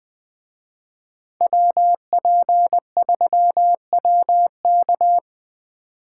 Morse code, heard as a keyed tone, about 19 wpm, 700 Hz. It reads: WP3WK